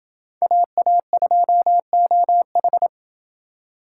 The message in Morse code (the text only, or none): AA2OH